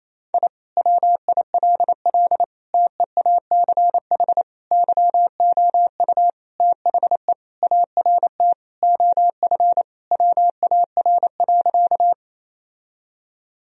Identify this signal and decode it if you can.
Morse code, heard as a keyed tone, about 27 wpm, 700 Hz. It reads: I WILL TEACH YOU THE ART OF WAR.
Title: Sentences from Top 1000 Words - Review Part 3 - 28wpm